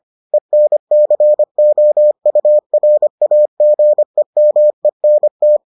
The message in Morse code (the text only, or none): ENCOURAGEMENT